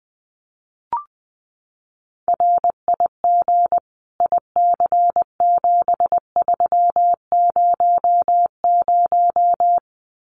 Morse code, heard as a keyed tone, about 20 wpm, 700 Hz. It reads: RIG IC7300
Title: Common QSO Phrases Encore 20wpm